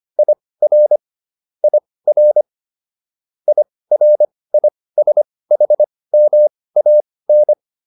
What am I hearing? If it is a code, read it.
Morse code, heard as a keyed tone, about 25 wpm, spaced as for 18 wpm, 600 Hz. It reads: IR IR IRISHMAN